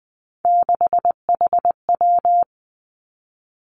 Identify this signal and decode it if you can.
Morse code, heard as a keyed tone, about 20 wpm, 700 Hz. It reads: 6HW
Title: Three Challenging Letter-Number Combinations 20wpm